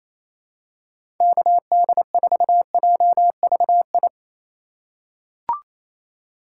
Morse code, heard as a keyed tone, about 28 wpm, 700 Hz. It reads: KD4JVI
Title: USA Call Signs - 3x Character Spacing - 28wpm